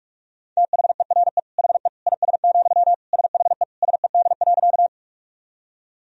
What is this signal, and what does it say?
Morse code, heard as a keyed tone, about 42 wpm, 700 Hz. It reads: THERE HE IS, SHE SED.
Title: Sentences from Top 100 Words and Abbreviations 45wpm